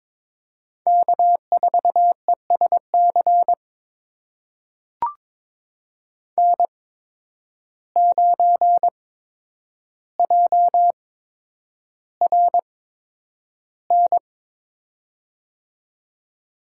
Morse code, heard as a keyed tone, about 22 wpm, spaced as for 9 wpm, 700 Hz. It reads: K4ESC N9JRN
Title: USA Call Signs - 5 Characters - 8x Character Spacing - 22wpm